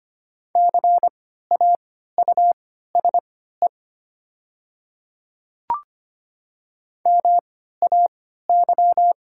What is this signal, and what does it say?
Morse code, heard as a keyed tone, about 25 wpm, spaced as for 14 wpm, 700 Hz. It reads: CAUSE MAY
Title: 3x Character Spacing - Top 200 Words - 25wpm